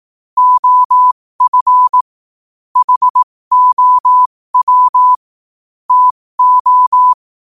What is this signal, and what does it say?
Morse code, heard as a keyed tone, about 18 wpm, spaced as for 15 wpm, 1000 Hz. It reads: OF HOW TO
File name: qotdfiles / 15wpm